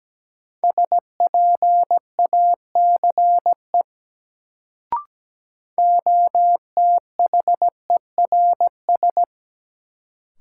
Morse code, heard as a keyed tone, about 17 wpm, 700 Hz. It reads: SPACE OTHERS
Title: Sentences from Top 1500 Words - New Word Drill Alt - 17wpm